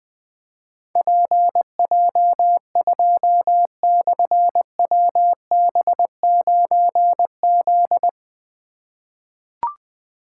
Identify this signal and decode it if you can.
Morse code, heard as a keyed tone, about 20 wpm, 700 Hz. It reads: PJ2/WB9Z
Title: Difficult Call Signs 20wpm